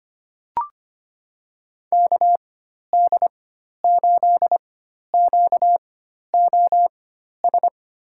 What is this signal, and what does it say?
Morse code, heard as a keyed tone, about 25 wpm, spaced as for 12 wpm, 700 Hz. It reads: KD8QOS
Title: USA Call Signs - 6 Characters - 4x Character Spacing - 25wpm